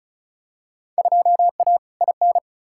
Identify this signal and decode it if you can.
Morse code, heard as a keyed tone, about 36 wpm, 700 Hz. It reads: 2A IN
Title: Field Day Exchanges - 35wpm